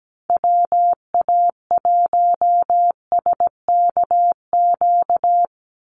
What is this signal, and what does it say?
Morse code, heard as a keyed tone, about 17 wpm, 700 Hz. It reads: WA1SKQ
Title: Call Signs Repeated 2x - 17wpm